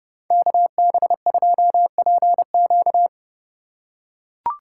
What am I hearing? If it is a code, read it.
Morse code, heard as a keyed tone, about 29 wpm, 700 Hz. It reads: KB2PQ